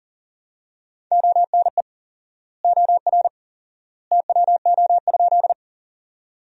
Morse code, heard as a keyed tone, about 40 wpm, 700 Hz. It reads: ONE OR TWO?
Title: 4x Word-Spacing - Sentences from Top 100 Words - 40wpm